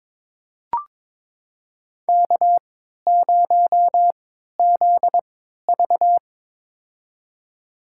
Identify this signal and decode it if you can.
Morse code, heard as a keyed tone, about 22 wpm, spaced as for 13 wpm, 700 Hz. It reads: K0ZV
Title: USA Call Signs - 3x Character Spacing - 22wpm